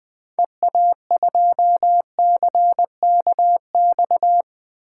Morse code, heard as a keyed tone, about 20 wpm, 700 Hz. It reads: EA2CKX
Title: 2023 SOTA Chaser Call Signs - 20wpm